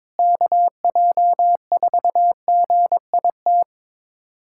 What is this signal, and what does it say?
Morse code, heard as a keyed tone, about 22 wpm, 700 Hz. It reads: KJ4GIT
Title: USA Callsign DE Callsign - 6 Characters - 22wpm